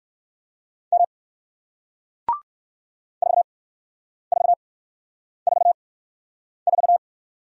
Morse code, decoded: M 4444

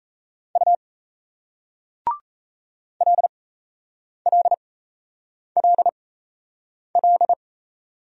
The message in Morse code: U LLLL

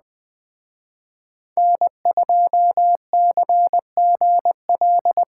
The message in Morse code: N2CGL

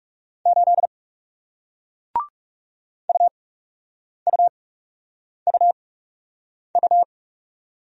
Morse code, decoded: 8 UUUU

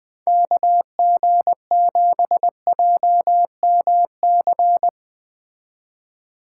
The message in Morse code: KG7JMC

